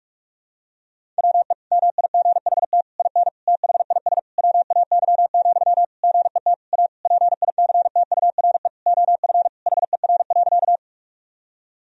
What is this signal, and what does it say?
Morse code, heard as a keyed tone, about 46 wpm, 700 Hz. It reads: WE MIGHT IN THIS WAY, GET A PICTURE OF HER.
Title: Sentences from Top 1000 Words - Review Part 1 - 45wpm